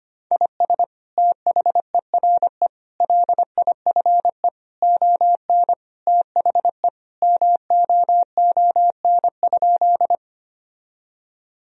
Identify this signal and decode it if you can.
Morse code, heard as a keyed tone, about 25 wpm, 700 Hz. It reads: IS THERE LIFE ON THE MOON?
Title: Sentences from Top 1000 Words - Review Part 3 - 25wpm